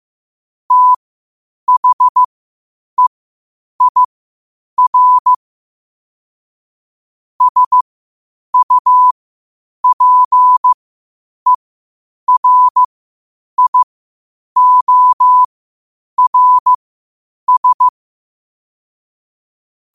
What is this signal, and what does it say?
Morse code, heard as a keyed tone, about 15 wpm, spaced as for 8 wpm, 1000 Hz. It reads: THEIR SUPERIORS
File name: qotdfiles / 8wpm